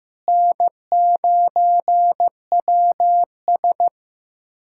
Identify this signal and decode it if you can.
Morse code, heard as a keyed tone, about 15 wpm, 700 Hz. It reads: N9WS
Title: USA Callsign DE Callsign - 4 Characters - 15wpm